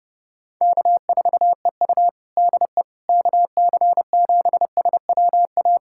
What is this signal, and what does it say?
Morse code, heard as a keyed tone, about 29 wpm, 700 Hz. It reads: K4EU DE KC7SWA
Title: Practice QSO - Exchange 4 - Ending - 30wpm